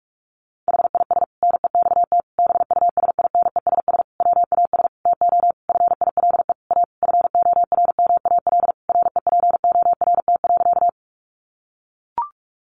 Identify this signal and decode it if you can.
Morse code, heard as a keyed tone, about 44 wpm, 700 Hz. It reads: HIS NEXT BUSINESS WAS TO FILE A FORMAL REPORT.
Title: Sentences from Top 1300 Words 45wpm